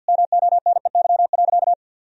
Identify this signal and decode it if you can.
Morse code, heard as a keyed tone, about 50 wpm, 700 Hz. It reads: MONEY.
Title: Sentences from Top 1000 Words - Review Part 4 - 50wpm